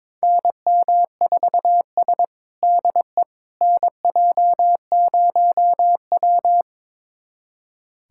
Morse code, heard as a keyed tone, about 22 wpm, 700 Hz. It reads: NM4S DE NJ0W
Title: USA Callsign DE Callsign - 4 Characters - 22wpm